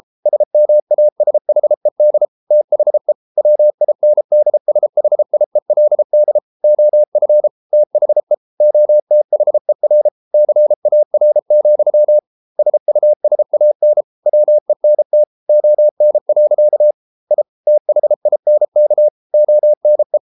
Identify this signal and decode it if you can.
Morse code, heard as a keyed tone, about 33 wpm, 600 Hz. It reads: SMASHED THE WINDSHIELD OF THE OTHER CAR, SUSAN WENT ON. I THINK ONE